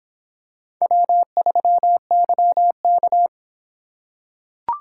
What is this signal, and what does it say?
Morse code, heard as a keyed tone, about 23 wpm, 700 Hz. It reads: W3YK